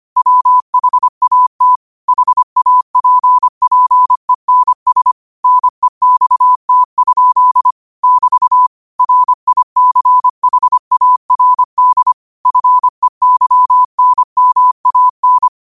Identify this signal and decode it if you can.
Morse code, heard as a keyed tone, about 25 wpm, 1000 Hz. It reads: WHAT HAPPENS NEXT? <BT> RICHARD FEYNMAN